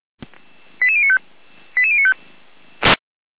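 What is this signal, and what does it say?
Five-tone selective call (ZVEI), heard as quick runs of five tones, tones of about 70 ms. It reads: 88095 88095
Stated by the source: ZVEI five-tone call 88098 received by SDR.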